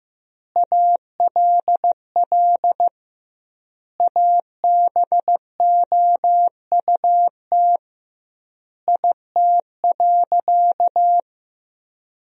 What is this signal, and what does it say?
Morse code, heard as a keyed tone, about 15 wpm, 700 Hz. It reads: ALL ABOUT IT.
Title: 2x Word-Spacing - Sentences from Top 400 Words - 15wpm